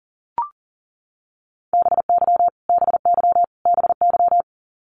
Morse code, heard as a keyed tone, about 40 wpm, 700 Hz. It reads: BY BY BY
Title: Top 500 Words Repeated 40wpm